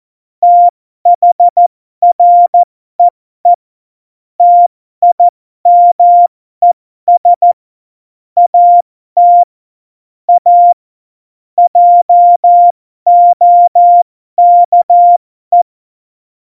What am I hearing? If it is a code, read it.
Morse code, heard as a keyed tone, about 14 wpm, spaced as for 12 wpm, 700 Hz. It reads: THREE TIMES AT A JOKE
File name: Morsekurs / Witze